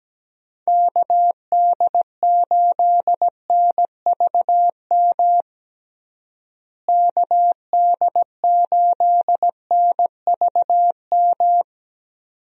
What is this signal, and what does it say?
Morse code, heard as a keyed tone, about 17 wpm, 700 Hz. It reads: KD8NVM KD8NVM
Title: USA Call Signs - 6 Characters - Repeated 2x - 17wpm